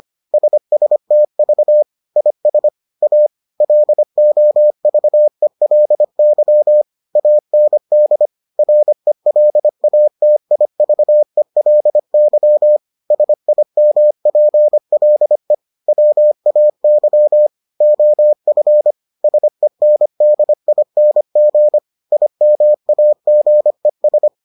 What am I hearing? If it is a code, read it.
Morse code, heard as a keyed tone, about 25 wpm, 600 Hz. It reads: SSTV IS A LOVELY AND RELATIVELY SIMPLE WAY OF SENDING IMAGES